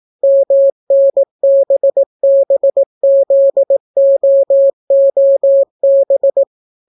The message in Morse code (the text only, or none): MNBBZOOB